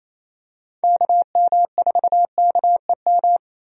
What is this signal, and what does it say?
Morse code, heard as a keyed tone, about 28 wpm, 700 Hz. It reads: KM4KEM
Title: USA Call Signs - 6 Characters - 28wpm